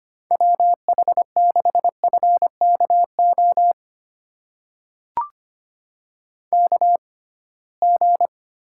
Morse code, heard as a keyed tone, about 25 wpm, spaced as for 19 wpm, 700 Hz. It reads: WH6FKO KG
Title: USA Call Signs - 6 Characters - 6x Character Spacing - 25wpm